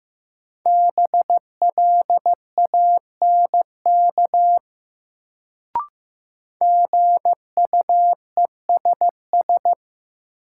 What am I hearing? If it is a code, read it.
Morse code, heard as a keyed tone, about 15 wpm, 700 Hz. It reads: BLANK GUESS
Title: Sentences from Top 1500 Words - New Word Drill Alt - 15wpm